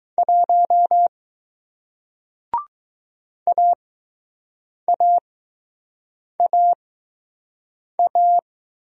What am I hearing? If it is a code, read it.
Morse code, heard as a keyed tone, about 21 wpm, spaced as for 6 wpm, 700 Hz. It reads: 1 AAAA